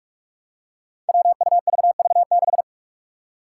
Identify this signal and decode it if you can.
Morse code, heard as a keyed tone, about 44 wpm, 700 Hz. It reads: WUVV6
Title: Five Challenging Letter-Number Combinations 45wpm